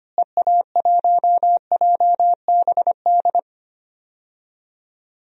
Morse code, heard as a keyed tone, about 25 wpm, 700 Hz. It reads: EA1JBD
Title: DXCC-100 Call Signs - 6 Characters - Repeated 2x - 25wpm